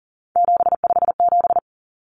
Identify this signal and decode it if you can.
Morse code, heard as a keyed tone, about 40 wpm, 700 Hz. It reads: 757
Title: Common Cut Numbers - 3-Digit 40wpm